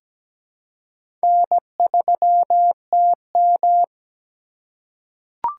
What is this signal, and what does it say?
Morse code, heard as a keyed tone, about 17 wpm, 700 Hz. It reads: N3TM
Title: USA Call Signs - 4 Characters - 2x Character Spacing - 17wpm